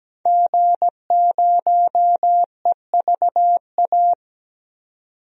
Morse code, heard as a keyed tone, about 17 wpm, 700 Hz. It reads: G0EVA
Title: DXCC-100 Call Signs - 5 Characters - 17wpm